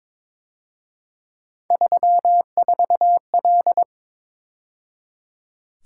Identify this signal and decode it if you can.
Morse code, heard as a keyed tone, about 26 wpm, 700 Hz. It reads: 34L
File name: Speed-Racing - Three Letter-Number Combinations - 22wpm